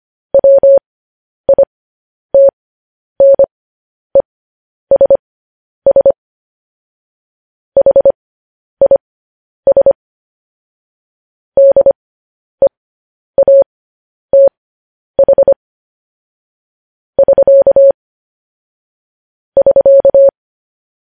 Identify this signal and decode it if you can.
Morse code, heard as a keyed tone, about 25 wpm, spaced as for 10 wpm, 550 Hz. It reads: WITNESS HIS DEATH <SK> <SK>
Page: News Headlines in Morse Code at 10 WPM